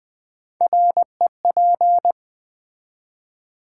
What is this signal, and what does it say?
Morse code, heard as a keyed tone, about 20 wpm, 700 Hz. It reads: REP
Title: Build A Word - 3 Letters At A Time - 6-Letter Words - 20wpm